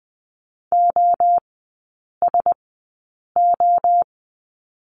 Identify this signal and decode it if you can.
Morse code, heard as a keyed tone, about 20 wpm, spaced as for 8 wpm, 700 Hz. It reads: OSO
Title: 053 Lesson 10e - Letter r - Three Characters 20wpm